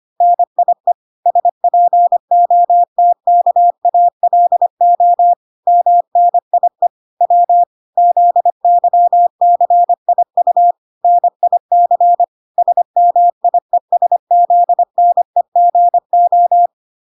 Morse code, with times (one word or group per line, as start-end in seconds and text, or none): 0.2-0.9 NIE
1.3-5.3 SPOTKALO
5.7-6.9 MNIE
7.2-7.6 W
8.0-10.7 ZYCIU
11.0-12.2 NIC
12.6-16.7 SMIESZNEGO